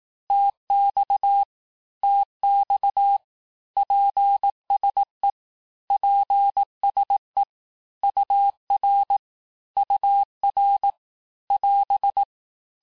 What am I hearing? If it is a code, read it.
Morse code, heard as a keyed tone, about 18 wpm, 800 Hz. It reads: TX TX PSE PSE UR UR <AS>
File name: teste abreviations vitesse 17mpm puis 22mpm